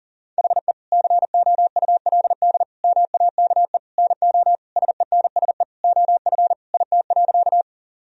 Morse code, heard as a keyed tone, about 40 wpm, 700 Hz. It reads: HE COULD MAKE NO SENSE OF IT.
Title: Sentences from Top 1000 Words - Review Part 4 - 40wpm